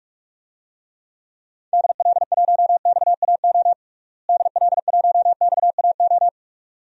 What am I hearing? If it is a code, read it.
Morse code, heard as a keyed tone, about 46 wpm, 700 Hz. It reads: DL1XAO DL1XAO
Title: DXCC-100 Call Signs - 6 Characters - Repeated 2x - 45wpm